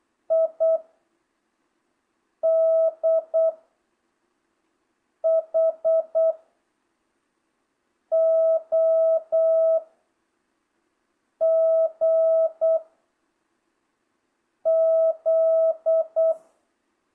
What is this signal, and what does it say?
Morse code, heard as a keyed tone, about 8 wpm, 660 Hz. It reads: I D H O G Z